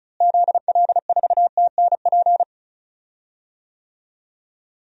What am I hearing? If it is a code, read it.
Morse code, heard as a keyed tone, about 36 wpm, 700 Hz. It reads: ZL4TNP
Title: DXCC-100 Call Signs - 6 Characters - Repeated 3x - 35wpm